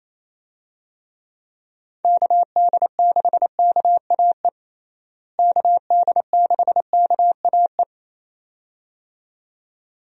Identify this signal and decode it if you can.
Morse code, heard as a keyed tone, about 28 wpm, 700 Hz. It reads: KD6KAE KD6KAE
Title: USA Call Signs - 6 Characters - Repeated 2x - 28wpm